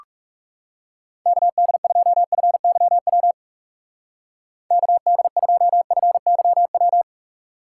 Morse code, heard as a keyed tone, about 36 wpm, 700 Hz. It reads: KD2FYW KD2FYW
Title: Speed Racing - USA 2x3 Call Signs 30wpm